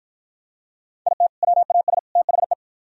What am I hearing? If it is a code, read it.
Morse code, heard as a keyed tone, about 49 wpm, 700 Hz. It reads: IT WAS THE